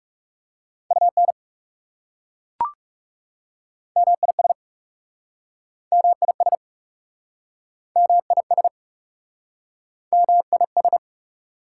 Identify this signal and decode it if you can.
Morse code, heard as a keyed tone, about 40 wpm, 700 Hz. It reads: UN MIS MIS MIS MIS